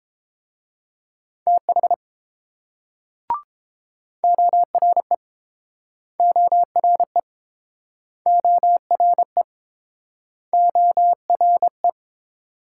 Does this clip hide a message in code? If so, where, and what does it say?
Yes: Morse code from 1.5 to 11.9 s, heard as a keyed tone, about 29 wpm, 700 Hz. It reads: TH ORE ORE ORE ORE